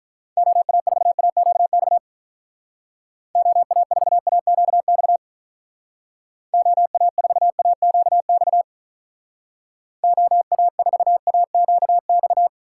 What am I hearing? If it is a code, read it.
Morse code, heard as a keyed tone, about 44 wpm, 700 Hz. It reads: OA4AQX OA4AQX OA4AQX OA4AQX
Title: Speed Racing - USA 2x3 Call Signs 35wpm